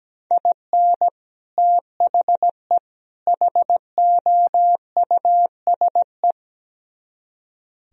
Morse code, heard as a keyed tone, about 17 wpm, 700 Hz. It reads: IN THE HOUSE
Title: Speed-Racing - Sets of 3 Words 17wpm